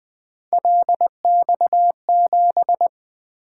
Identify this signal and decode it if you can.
Morse code, heard as a keyed tone, about 24 wpm, 700 Hz. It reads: LX7
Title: Speed-Racing - Three Letter-Number Combinations - 20wpm